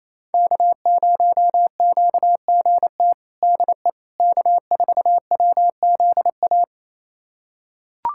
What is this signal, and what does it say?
Morse code, heard as a keyed tone, about 27 wpm, 700 Hz. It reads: K0QGT DE K4WZA
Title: USA Callsign DE Callsign - 5 Characters - 28wpm